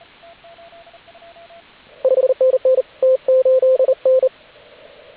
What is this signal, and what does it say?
Morse code, heard as a keyed tone, about 32 wpm, 520 Hz, with radio noise. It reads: 5NN T8N
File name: AUDIO2019 / HG8DX CW